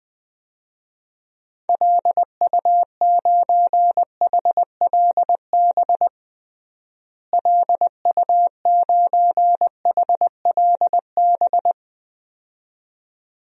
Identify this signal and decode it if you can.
Morse code, heard as a keyed tone, about 20 wpm, 700 Hz. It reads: LU9HLB LU9HLB
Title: DXCC-100 Call Signs - 6 Characters - Repeated 2x - 20wpm